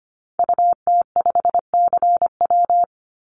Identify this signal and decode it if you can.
Morse code, heard as a keyed tone, about 25 wpm, 700 Hz. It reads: UT5CW
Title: Call Signs Repeated 2x - 25wpm